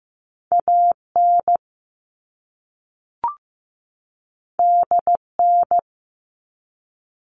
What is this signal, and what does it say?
Morse code, heard as a keyed tone, about 15 wpm, 700 Hz. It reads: AN DN
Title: All Cut Numbers - 2-Digit 15wpm